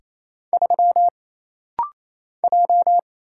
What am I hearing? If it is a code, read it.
Morse code, heard as a keyed tone, about 28 wpm, 700 Hz. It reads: 3 J
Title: Single Letter-Number Alt - Rapid-Fire 28wpm